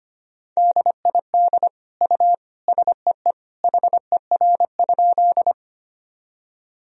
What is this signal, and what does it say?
Morse code, heard as a keyed tone, about 25 wpm, 700 Hz. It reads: DID U SEE HER?
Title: Sentences from Top 100 Words and Abbreviations 25wpm